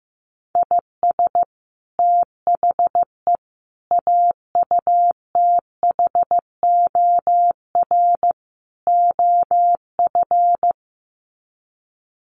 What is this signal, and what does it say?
Morse code, heard as a keyed tone, about 15 wpm, 700 Hz. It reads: IS THE AUTHOR OF
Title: Sets of 4 Words 15wpm